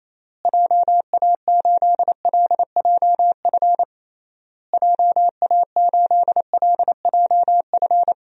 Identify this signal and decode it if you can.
Morse code, heard as a keyed tone, about 28 wpm, 700 Hz. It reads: JA8LJF JA8LJF
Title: DXCC-100 Call Signs - 6 Characters - Repeated 3x - 28wpm